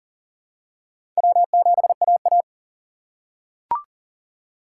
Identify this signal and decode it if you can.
Morse code, heard as a keyed tone, about 40 wpm, 700 Hz. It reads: W7AA
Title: USA Call Signs - 4 Characters - 2x Character Spacing - 40wpm